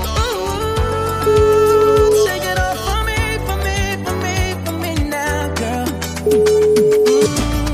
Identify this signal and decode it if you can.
Ringback tone (Europe, 425 Hz), with noise.